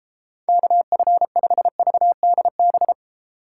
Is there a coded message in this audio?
Yes: Morse code from 0.5 to 2.9 s, heard as a keyed tone, about 37 wpm, 700 Hz. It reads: KF5VDB